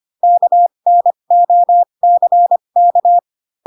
Morse code, heard as a keyed tone, about 25 wpm, spaced as for 22 wpm, 700 Hz. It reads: KNOCK
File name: Pinocchio / Pinocchio ch2 18 25 ik1hge